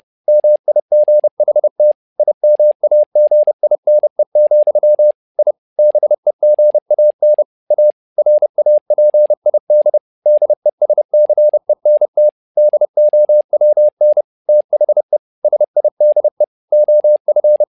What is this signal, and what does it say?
Morse code, heard as a keyed tone, about 30 wpm, 600 Hz. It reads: MIGHT IMAGINE, I BEGAN A RAPID DESCENT DOWN THE SIDE OF
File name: learning center / tower trouble / tower trouble 30wpm